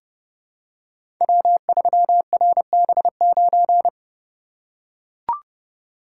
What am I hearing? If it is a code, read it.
Morse code, heard as a keyed tone, about 30 wpm, 700 Hz. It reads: W3RB9
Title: Code Groups - 5 Letter:Numbers - 30wpm